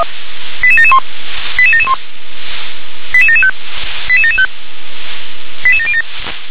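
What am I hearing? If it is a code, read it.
Five-tone selective call (ZVEI), heard as quick runs of five tones, tones of about 70 ms. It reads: 88771 88771 88775 88775 88707
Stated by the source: ZVEI five-tone selective call, received by SDR.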